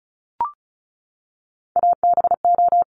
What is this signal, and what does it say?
Morse code, heard as a keyed tone, about 36 wpm, 700 Hz. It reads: ABO
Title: Top Three Letter Combinations in Words 35wpm